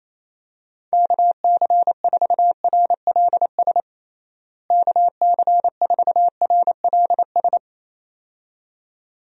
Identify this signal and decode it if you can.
Morse code, heard as a keyed tone, about 28 wpm, 700 Hz. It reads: KC4RLS KC4RLS
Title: USA Call Signs - 6 Characters - Repeated 2x - 28wpm